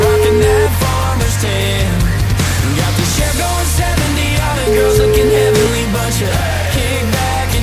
Ringback tone (Europe, 425 Hz), with noise.